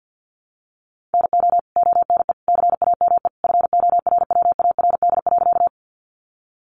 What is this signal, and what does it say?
Morse code, heard as a keyed tone, about 50 wpm, 700 Hz. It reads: NO ONE CAME FORWARD.
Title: Sentences from Top 1300 Words 50wpm